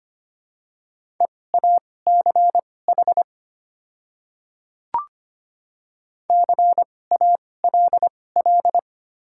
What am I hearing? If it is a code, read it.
Morse code, heard as a keyed tone, about 25 wpm, spaced as for 18 wpm, 700 Hz. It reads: EACH CALL
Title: 2x Character Spacing - Top 100 Words - 25wpm